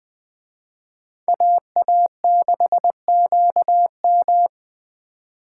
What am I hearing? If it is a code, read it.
Morse code, heard as a keyed tone, about 20 wpm, 700 Hz. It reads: AA6QM
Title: Speed Racing - USA 2x2 Call Signs 17wpm